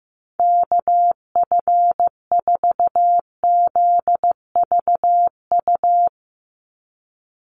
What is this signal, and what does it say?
Morse code, heard as a keyed tone, about 15 wpm, 700 Hz. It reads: KF4ZVU
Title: USA Call Signs 15wpm